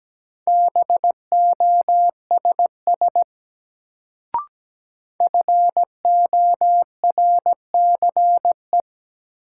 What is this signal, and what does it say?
Morse code, heard as a keyed tone, about 17 wpm, 700 Hz. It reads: BOSS FORCE
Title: Sentences from Top 700 Words - New Word Drill - 17wpm